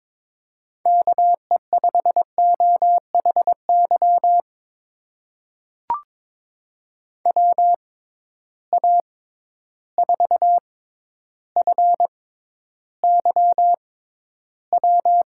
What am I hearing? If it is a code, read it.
Morse code, heard as a keyed tone, about 22 wpm, spaced as for 11 wpm, 700 Hz. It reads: KE5OHY WA4FYW
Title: USA Call Signs - 6 Characters - 6x Character Spacing - 22wpm